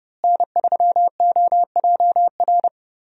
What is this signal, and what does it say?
Morse code, heard as a keyed tone, about 30 wpm, 700 Hz. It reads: N3OJR
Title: USA Call Signs - 5 Characters - Repeated 2x - 30wpm